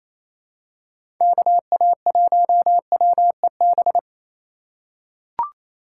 Morse code, heard as a keyed tone, about 28 wpm, 700 Hz. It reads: KA1WEB
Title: USA Call Signs - 6 Characters - 4x Character Spacing - 28wpm